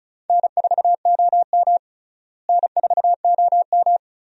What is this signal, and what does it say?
Morse code, heard as a keyed tone, about 36 wpm, 700 Hz. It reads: N4OM N4OM
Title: USA Call Signs - 4 Characters - Repeated 2x - 35wpm